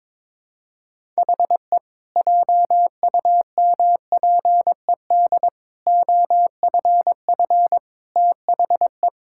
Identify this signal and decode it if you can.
Morse code, heard as a keyed tone, about 22 wpm, 700 Hz. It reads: HE JUMPED OFF THE
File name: practice / 22wpm / Sentences with 2-Word Phrases - 22wpm